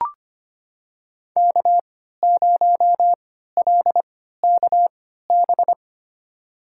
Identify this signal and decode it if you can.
Morse code, heard as a keyed tone, about 25 wpm, spaced as for 14 wpm, 700 Hz. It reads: K0LKB
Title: USA Call Signs - 5 Characters - 3x Character Spacing - 25wpm